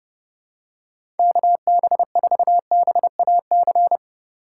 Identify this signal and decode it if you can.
Morse code, heard as a keyed tone, about 30 wpm, 700 Hz. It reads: KB4BAC
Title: USA Call Signs - 6 Characters - 4x Character Spacing - 30wpm